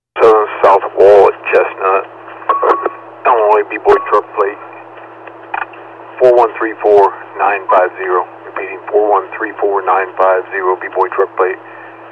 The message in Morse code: none

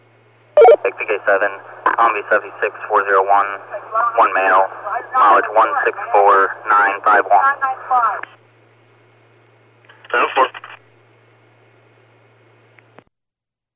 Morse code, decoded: none

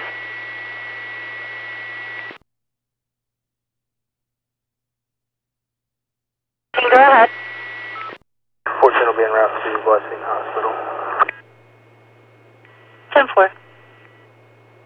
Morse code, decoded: none

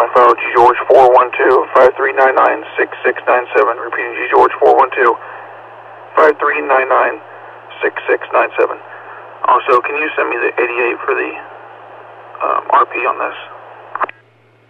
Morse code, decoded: none

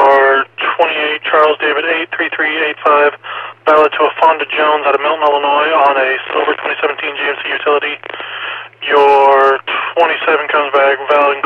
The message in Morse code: none

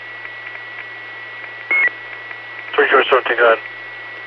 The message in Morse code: none

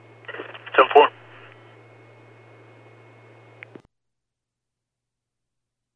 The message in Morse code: none

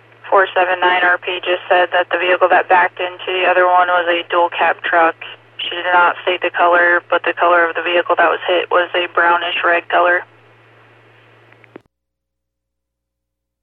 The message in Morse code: none

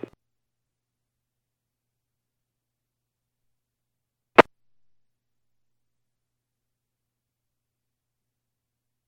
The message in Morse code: none

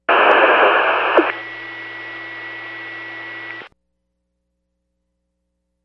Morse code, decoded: none

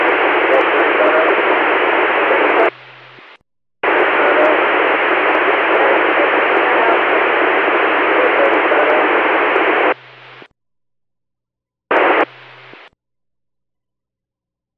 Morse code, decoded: none

